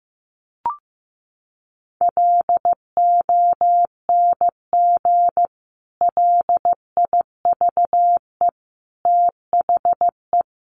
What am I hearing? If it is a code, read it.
Morse code, heard as a keyed tone, about 15 wpm, 700 Hz. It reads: LONG LIVE THE
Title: Sentences from Top 1500 Words 15wpm